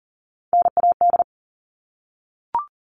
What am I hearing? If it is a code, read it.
Morse code, heard as a keyed tone, about 40 wpm, 700 Hz. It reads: NAD
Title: All Cut Numbers - 3-Digit 40wpm